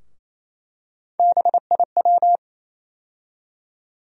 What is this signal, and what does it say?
Morse code, heard as a keyed tone, about 28 wpm, 700 Hz. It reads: BIW
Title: USA Call Sign Postfixes - 3 Letters 28wpm